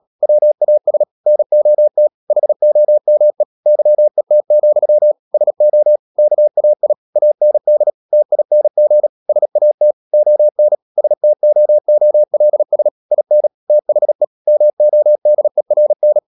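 Morse code, heard as a keyed tone, about 37 wpm, 600 Hz. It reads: WAS NOT HOME YET, SO KAI AND TING SAT ON STOOLS IN THE MODERN
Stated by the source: Morse code practice at 35–39 wpm sending English text of Ken Follett's novel Never, from CW med Gnister.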